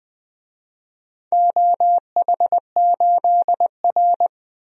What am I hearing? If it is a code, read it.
Morse code, heard as a keyed tone, about 20 wpm, 700 Hz. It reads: OH8R
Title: DXCC-100 Call Signs - 4 Characters - 20wpm